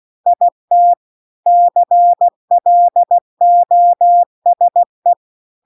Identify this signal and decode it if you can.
Morse code, heard as a keyed tone, about 16 wpm, 700 Hz. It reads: IT CLOSE